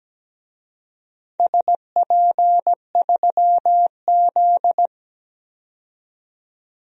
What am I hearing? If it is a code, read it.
Morse code, heard as a keyed tone, about 17 wpm, 700 Hz. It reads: SP3Z
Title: DXCC-100 Call Signs - 4 Characters - 17wpm